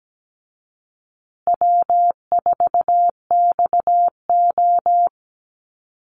Morse code, heard as a keyed tone, about 17 wpm, 700 Hz. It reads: W4XO
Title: Easy Call Signs Repeated 3x - 17wpm